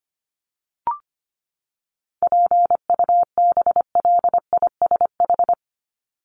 Morse code, heard as a keyed tone, about 25 wpm, 700 Hz. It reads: PUBLISH
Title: ED Suffix - Top 100 Words with Base Words 25wpm